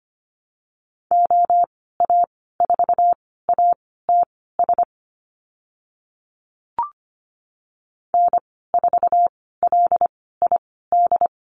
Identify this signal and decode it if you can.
Morse code, heard as a keyed tone, about 25 wpm, spaced as for 16 wpm, 690 Hz. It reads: OA4ATS N4LID
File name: Lesson 15g - Number 5 - Call Signs 25wpm